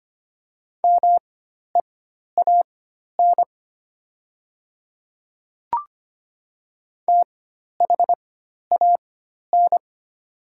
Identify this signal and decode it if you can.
Morse code, heard as a keyed tone, about 25 wpm, spaced as for 12 wpm, 700 Hz. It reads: MEAN THAN
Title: 4x Character Spacing - Top 300 Words - 25wpm